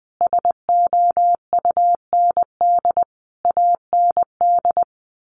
Morse code, heard as a keyed tone, about 20 wpm, 700 Hz. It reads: SOUND AND
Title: Sentences from Top 1400 Words 20wpm